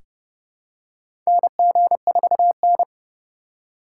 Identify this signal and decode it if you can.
Morse code, heard as a keyed tone, about 32 wpm, 700 Hz. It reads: NG4N
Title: USA Call Signs - 2x Character Spacing - 30wpm